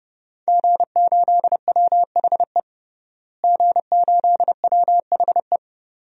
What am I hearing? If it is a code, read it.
Morse code, heard as a keyed tone, about 31 wpm, 700 Hz. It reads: G8WHE G8WHE
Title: DXCC-100 Call Signs - 5 Characters - Repeated 2x - 30wpm